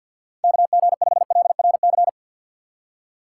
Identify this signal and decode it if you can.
Morse code, heard as a keyed tone, about 52 wpm, 700 Hz. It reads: KG5LRC
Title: USA Call Signs - 6 Characters - 4x Character Spacing - 50wpm